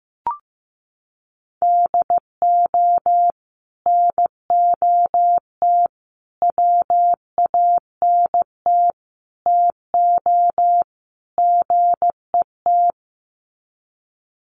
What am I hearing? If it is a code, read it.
Morse code, heard as a keyed tone, about 15 wpm, 700 Hz. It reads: DO NOT WANT TO GET
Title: Sets of 5 Words 15wpm